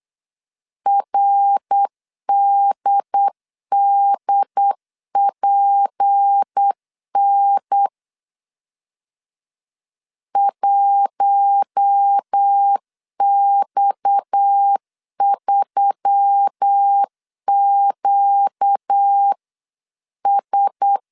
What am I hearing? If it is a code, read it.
Morse code, heard as a keyed tone, about 8 wpm, 800 Hz. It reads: RDDPN 1X3QS
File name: 8wpm